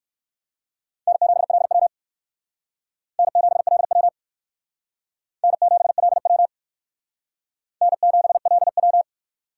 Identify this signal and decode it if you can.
Morse code, heard as a keyed tone, about 67 wpm, 700 Hz. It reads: N7LW N7LW N7LW N7LW